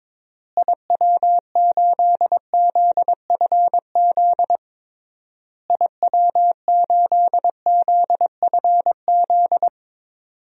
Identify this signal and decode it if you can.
Morse code, heard as a keyed tone, about 22 wpm, 700 Hz. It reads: IW8ZFZ IW8ZFZ
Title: DXCC-100 Call Signs - 6 Characters - Repeated 3x - 22wpm